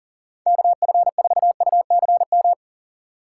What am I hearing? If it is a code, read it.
Morse code, heard as a keyed tone, about 40 wpm, 700 Hz. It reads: KF4UCM
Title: USA Call Signs - 6 Characters - 8x Character Spacing - 40wpm